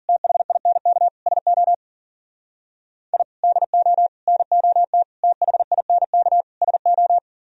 Morse code, heard as a keyed tone, about 52 wpm, 700 Hz. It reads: THINK SO I DO NOT THINK SO